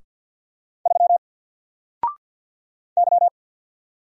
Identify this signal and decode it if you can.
Morse code, heard as a keyed tone, about 48 wpm, 700 Hz. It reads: 3 Y
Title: Single Letter-Number Alt 50wpm